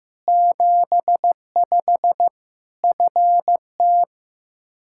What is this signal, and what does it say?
Morse code, heard as a keyed tone, about 15 wpm, 700 Hz. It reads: 75 FT
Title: Practice QSO - Exchange 2 - Info - 15wpm